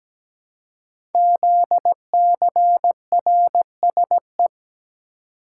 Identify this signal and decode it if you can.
Morse code, heard as a keyed tone, about 17 wpm, 700 Hz. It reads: ZCRSE
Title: Code Groups - 5 Letters - 17wpm